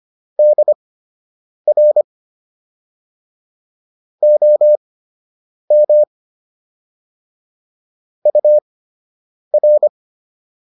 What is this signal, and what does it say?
Morse code, heard as a keyed tone, about 25 wpm, spaced as for 8 wpm, 600 Hz. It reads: DR OM UR